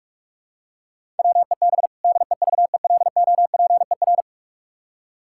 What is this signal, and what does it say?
Morse code, heard as a keyed tone, about 46 wpm, 700 Hz. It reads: WEB DEVELOPER
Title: Practice QSO - Exchange 3 - More Info - 45wpm